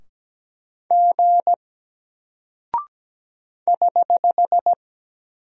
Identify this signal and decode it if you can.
Morse code, heard as a keyed tone, about 17 wpm, 700 Hz. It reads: G <HH>
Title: Single Character (Letter:Number:Punctuation:Prosign) Alt 17wpm